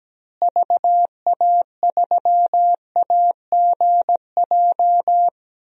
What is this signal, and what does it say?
Morse code, heard as a keyed tone, about 17 wpm, 700 Hz. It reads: VA3AGJ
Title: DXCC-100 Call Signs - 6 Characters - Repeated 3x - 17wpm